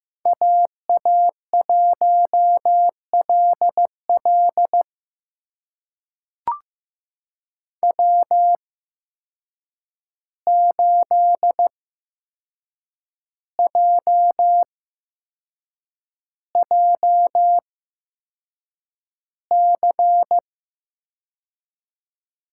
Morse code, heard as a keyed tone, about 15 wpm, spaced as for 6 wpm, 700 Hz. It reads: AA1LL W8JJC
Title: USA Call Signs - 5 Characters - 8x Character Spacing - 15wpm